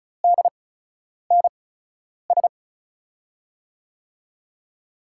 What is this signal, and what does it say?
Morse code, heard as a keyed tone, about 36 wpm, spaced as for 10 wpm, 700 Hz. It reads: DNS
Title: USA Call Signs - 5 Characters - 8x Character Spacing - 35wpm